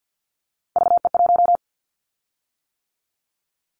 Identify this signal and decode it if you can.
Morse code, heard as a keyed tone, about 50 wpm, 700 Hz. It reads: VE1
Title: Callsign Prefixes for Top 100 DXCC - 50wpm